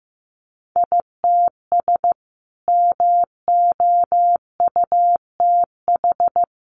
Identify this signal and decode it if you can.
Morse code, heard as a keyed tone, about 15 wpm, 700 Hz. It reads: ITS MOUTH
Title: A Sound Of Thunder 15wpm